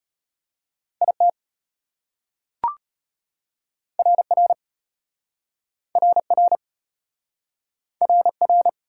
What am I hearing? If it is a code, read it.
Morse code, heard as a keyed tone, about 33 wpm, 700 Hz. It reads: IT RR RR RR